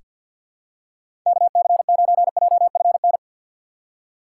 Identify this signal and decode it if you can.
Morse code, heard as a keyed tone, about 47 wpm, 700 Hz. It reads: KC9JFN
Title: USA Call Signs - 6 Characters - 4x Character Spacing - 50wpm